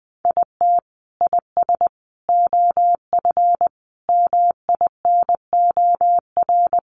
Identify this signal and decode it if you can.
Morse code, heard as a keyed tone, about 20 wpm, 700 Hz. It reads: IT IS OF MINOR
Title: Sentences from Top 1500 Words 20wpm